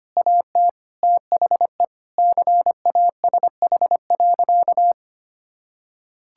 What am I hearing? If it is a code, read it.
Morse code, heard as a keyed tone, about 25 wpm, 700 Hz. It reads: AT THE CASH.
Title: Sentences from Top 1000 Words - Review Part 5 - 25wpm